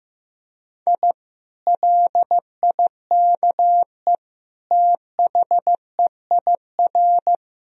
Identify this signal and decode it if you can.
Morse code, heard as a keyed tone, about 15 wpm, 700 Hz. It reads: I LIKE THEIR
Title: Sentences from Top 1000 Words - Review Part 4 - 15wpm